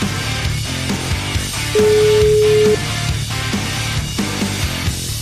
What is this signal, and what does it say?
Ringback tone (Europe, 425 Hz), with noise.